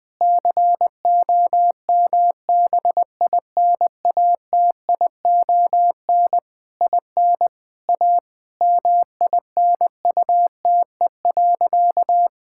Morse code, heard as a keyed tone, about 21 wpm, 700 Hz. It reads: COMBINATION IN A MINUTE.
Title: Sentences from Top 1000 Words - Review Part 6 - 20wpm